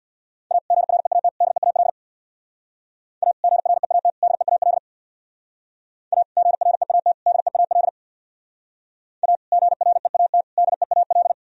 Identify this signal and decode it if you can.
Morse code, heard as a keyed tone, about 61 wpm, 700 Hz. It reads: A GREAT DEAL A GREAT DEAL A GREAT DEAL A GREAT DEAL
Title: Speed-Racing - Sets of 3 Words 50wpm